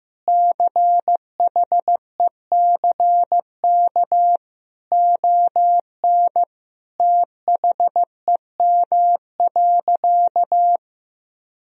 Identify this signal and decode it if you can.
Morse code, heard as a keyed tone, about 16 wpm, 700 Hz. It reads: CHECK ON THEM.
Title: Sentences from Top 1000 Words - Review Part 3 - 15wpm